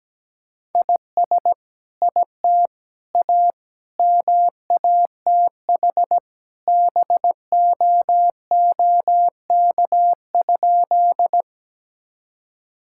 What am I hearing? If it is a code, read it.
Morse code, heard as a keyed tone, about 17 wpm, 700 Hz. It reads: IS IT A MATH BOOK?
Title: Sentences from Top 1000 Words - Review Part 6 - 17wpm